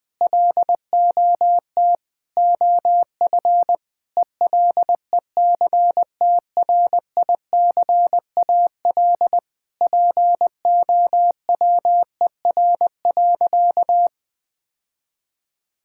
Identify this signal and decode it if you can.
Morse code, heard as a keyed tone, about 20 wpm, 700 Hz. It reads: LOT OF ELECTRICAL POWER.
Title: Sentences from Top 1000 Words - Review Part 6 - 20wpm